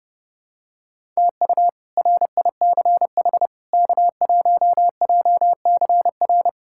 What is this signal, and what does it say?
Morse code, heard as a keyed tone, about 30 wpm, 700 Hz. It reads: TU RICH K1JCR
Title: K1USN SST QSO Practice - 30wpm